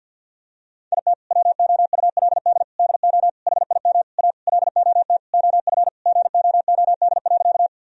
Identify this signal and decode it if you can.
Morse code, heard as a keyed tone, about 50 wpm, 700 Hz. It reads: IT WOULD DO HIM A LOT OF GOOD.
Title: Sentences from Top 1000 Words - Review Part 3 - 50wpm